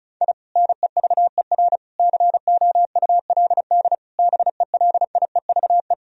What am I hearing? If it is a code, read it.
Morse code, heard as a keyed tone, about 34 wpm, 700 Hz. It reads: I NEVER COULD BELIEVE
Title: Sentences from Top 1000 Words - Review Part 1 - 35wpm